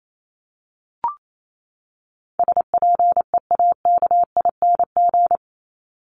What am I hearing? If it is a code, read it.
Morse code, heard as a keyed tone, about 28 wpm, 700 Hz. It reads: SPEAKING
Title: ING Suffix - Top 100 Words with Base Words 28wpm